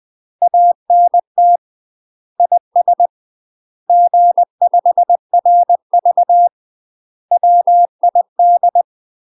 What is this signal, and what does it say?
Morse code, heard as a keyed tone, about 20 wpm, 700 Hz. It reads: ANT IS G5RV WID
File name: UK MASTER QSOs TRAINING / 20WPM / UK MASTER QSOs 0011